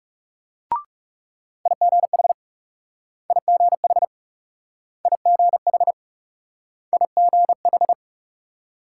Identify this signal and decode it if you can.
Morse code, heard as a keyed tone, about 41 wpm, 700 Hz. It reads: IGH IGH IGH IGH